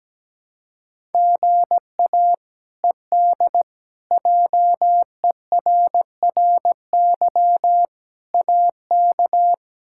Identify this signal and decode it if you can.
Morse code, heard as a keyed tone, about 17 wpm, 700 Hz. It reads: GA ED JERRY AK
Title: K1USN SST QSO Practice - 17wpm